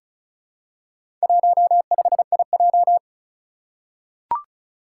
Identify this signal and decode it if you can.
Morse code, heard as a keyed tone, about 36 wpm, 700 Hz. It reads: 15IJ
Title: Four Challenging Letter-Number Combinations 35wpm